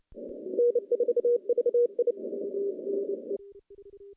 Morse code, heard as a keyed tone, about 29 wpm, 480 Hz. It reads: N4VI
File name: contests / CW-SS-k5tr- -0100